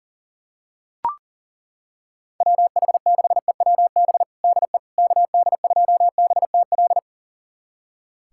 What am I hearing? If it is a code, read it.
Morse code, heard as a keyed tone, about 40 wpm, 700 Hz. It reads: WH6EWB DE KD2BTL
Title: USA Callsign DE Callsign - 6 Characters - 40wpm